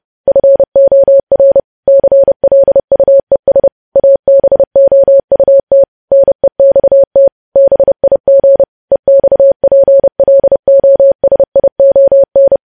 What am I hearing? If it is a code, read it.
Morse code, heard as a keyed tone, about 30 wpm, 550 Hz. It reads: FOR CLUES ABOUT NEXT BIG EXPLOSION
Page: News Headlines in Morse Code at 30 WPM